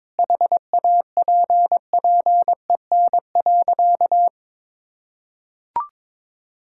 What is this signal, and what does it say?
Morse code, heard as a keyed tone, about 22 wpm, 700 Hz. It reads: HAPPEN.
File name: practice / 22wpm / Sentences with 3-Word Phrases - 22wpm